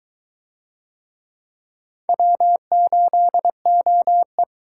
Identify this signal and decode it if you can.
Morse code, heard as a keyed tone, about 22 wpm, 700 Hz. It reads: W8OE